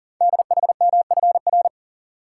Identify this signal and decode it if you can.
Morse code, heard as a keyed tone, about 43 wpm, 700 Hz. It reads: DHMFR
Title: Code Groups - 5 Letters - 40wpm